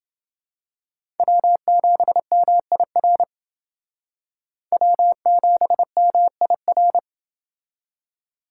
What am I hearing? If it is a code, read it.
Morse code, heard as a keyed tone, about 28 wpm, 700 Hz. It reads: W7MIR W7MIR